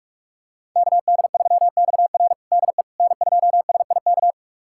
Morse code, heard as a keyed tone, about 42 wpm, 700 Hz. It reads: KD3XR DE N2SIK
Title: USA Callsign DE Callsign - 5 Characters - 45wpm